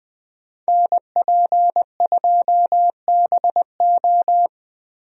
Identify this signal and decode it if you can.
Morse code, heard as a keyed tone, about 20 wpm, 700 Hz. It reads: NP2BO
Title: USA Call Signs - 5 Characters - Repeated 2x - 20wpm